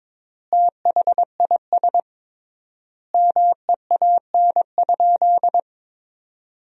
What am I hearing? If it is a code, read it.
Morse code, heard as a keyed tone, about 21 wpm, 700 Hz. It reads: THIS MEAN?
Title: 3x Word-Spacing - Sentences from Top 200 Words - 22wpm